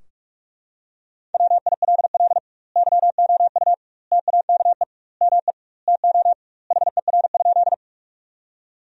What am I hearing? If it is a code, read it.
Morse code, heard as a keyed tone, about 44 wpm, 700 Hz. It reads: WILL YOU TAKE ME TO HER?
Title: 2x Word-Spacing - Sentences from Top 200 Words - 45wpm